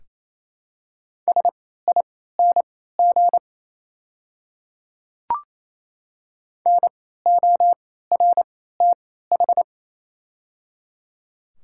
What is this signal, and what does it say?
Morse code, heard as a keyed tone, about 28 wpm, spaced as for 16 wpm, 700 Hz. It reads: SING NORTH
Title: 3x Character Spacing - Top 500 Words - 28wpm